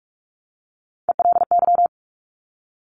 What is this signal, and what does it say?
Morse code, heard as a keyed tone, about 46 wpm, 700 Hz. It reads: ELY
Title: Top Three Letter Combinations in Words 45wpm